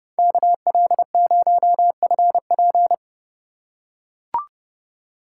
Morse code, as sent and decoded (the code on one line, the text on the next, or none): -.- .-.. ----- ..-. .--.
KL0FP